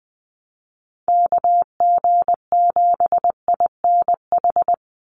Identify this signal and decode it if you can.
Morse code, heard as a keyed tone, about 19 wpm, 700 Hz. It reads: KG7INH
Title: USA Call Signs 20wpm